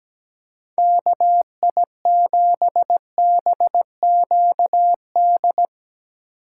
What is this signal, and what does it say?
Morse code, heard as a keyed tone, about 21 wpm, 700 Hz. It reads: KI7BQD
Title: Speed Racing - USA 2x3 Call Signs 17wpm